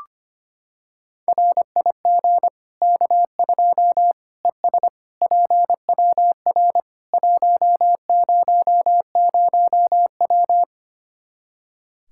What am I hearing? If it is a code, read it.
Morse code, heard as a keyed tone, about 25 wpm, 700 Hz. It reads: RIG K2 ES PWR 100W
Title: K1USN SST QSO Practice 25wpm